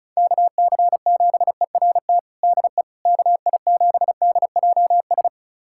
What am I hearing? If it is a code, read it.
Morse code, heard as a keyed tone, about 36 wpm, 690 Hz. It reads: KC7ERT DE KI7DJS
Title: USA Callsign DE Callsign - 6 Characters - 35wpm